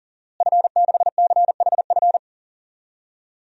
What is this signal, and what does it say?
Morse code, heard as a keyed tone, about 40 wpm, 700 Hz. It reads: F6CHF
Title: DXCC-100 Call Signs - 5 Characters - Repeated 3x - 40wpm